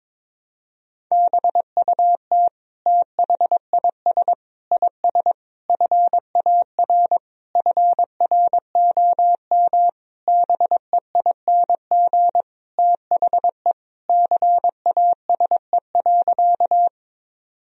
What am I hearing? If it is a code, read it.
Morse code, heard as a keyed tone, about 22 wpm, 700 Hz. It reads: BUT THIS IS FAR FROM BEING THE CASE.
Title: Sentences from Top 1000 Words - Review Part 3 - 22wpm